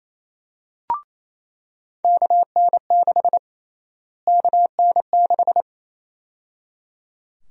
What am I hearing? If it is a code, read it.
Morse code, heard as a keyed tone, about 28 wpm, 700 Hz. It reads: KN6 KN6
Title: USA Call Sign Prefixes - Repeated 2x - 28wpm